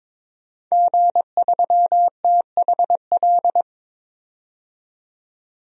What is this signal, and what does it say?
Morse code, heard as a keyed tone, about 22 wpm, 700 Hz. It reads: G3THL
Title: DXCC-100 Call Signs - 5 Characters - 22wpm